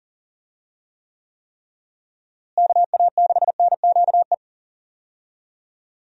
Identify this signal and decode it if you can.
Morse code, heard as a keyed tone, about 40 wpm, 700 Hz. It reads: KA6NQE
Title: USA Call Signs - 6 Characters - Repeated 3x - 40wpm